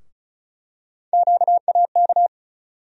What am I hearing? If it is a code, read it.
Morse code, heard as a keyed tone, about 34 wpm, 700 Hz. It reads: QAK
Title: USA Call Sign Postfixes - 3 Letters 35wpm